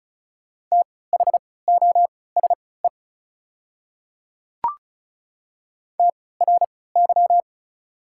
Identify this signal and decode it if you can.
Morse code, heard as a keyed tone, about 36 wpm, spaced as for 20 wpm, 690 Hz. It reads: THOSE TRY
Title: 3x Character Spacing - Top 300 Words - 35wpm